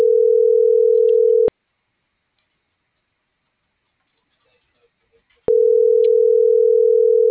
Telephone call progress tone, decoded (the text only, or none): ringback tone